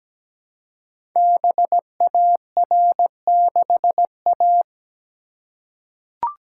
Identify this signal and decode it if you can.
Morse code, heard as a keyed tone, about 17 wpm, 700 Hz. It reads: BAR6A
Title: Code Groups - 5 Letter:Numbers - 17wpm